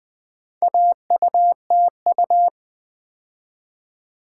Morse code, heard as a keyed tone, about 20 wpm, 700 Hz. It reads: AUTU
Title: Build A Word - 2 Letters At A Time - 6-Letter Words - 20wpm